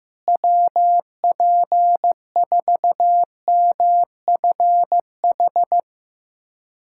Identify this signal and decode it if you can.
Morse code, heard as a keyed tone, about 16 wpm, 700 Hz. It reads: WP4MFH
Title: SOTA QSO Practice 15wpm